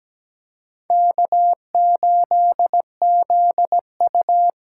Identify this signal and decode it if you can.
Morse code, heard as a keyed tone, about 20 wpm, 700 Hz. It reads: K8ZU
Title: Speed Racing - USA 1x2 Call Signs 17wpm